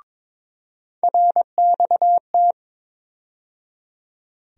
Morse code, heard as a keyed tone, about 22 wpm, 700 Hz. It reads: RXT
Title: USA Call Sign Postfixes - 3 Letters 22wpm